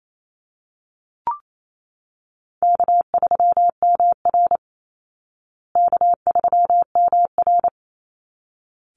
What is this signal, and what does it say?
Morse code, heard as a keyed tone, about 28 wpm, 700 Hz. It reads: K3MR K3MR
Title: Call Signs Repeated 3x - 28wpm